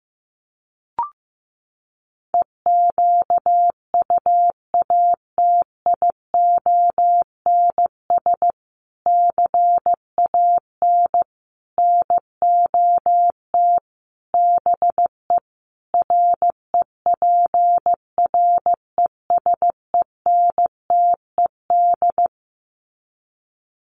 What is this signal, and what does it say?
Morse code, heard as a keyed tone, about 15 wpm, 700 Hz. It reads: EQUATIONS CAN NOT BE REPRESENTED
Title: Sets of 5 Words 15wpm